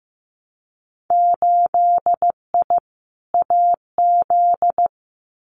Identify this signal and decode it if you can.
Morse code, heard as a keyed tone, about 15 wpm, 700 Hz. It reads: 8I AZ
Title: Winter Field Day Exchanges Encore 15wpm